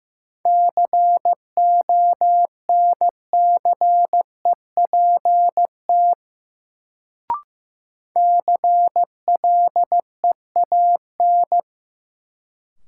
Morse code, heard as a keyed tone, about 15 wpm, 700 Hz. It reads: CONCEPT CLEAN
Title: Sentences from Top 1100 Words - New Word Drill - 15wpm